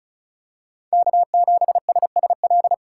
Morse code, heard as a keyed tone, about 36 wpm, 700 Hz. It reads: K7SSL
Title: USA Call Signs - 2x Character Spacing - 35wpm